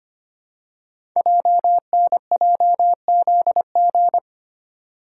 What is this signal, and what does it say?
Morse code, heard as a keyed tone, about 25 wpm, 700 Hz. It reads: JNJZG
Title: Code Groups - 5 Letters - 25wpm